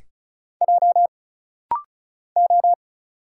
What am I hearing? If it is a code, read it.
Morse code, heard as a keyed tone, about 35 wpm, 700 Hz. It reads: J O
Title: Single Letters - Rapid-Fire 35wpm